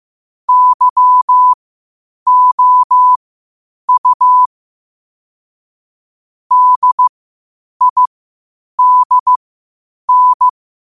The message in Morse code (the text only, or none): YOU DIDN